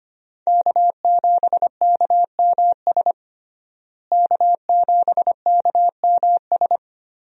Morse code, heard as a keyed tone, about 25 wpm, 700 Hz. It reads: K7KMS K7KMS
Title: USA Call Signs - 5 Characters - Repeated 3x - 25wpm